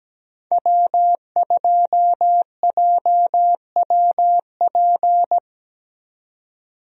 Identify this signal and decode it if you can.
Morse code, heard as a keyed tone, about 17 wpm, 700 Hz. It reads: W2JWP
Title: USA Call Signs - 5 Characters - 17wpm